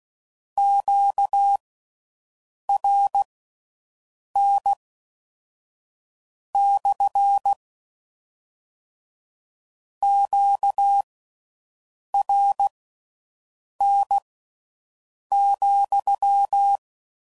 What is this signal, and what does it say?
Morse code, heard as a keyed tone, about 16 wpm, spaced as for 6 wpm, 800 Hz. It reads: QRN / QRN,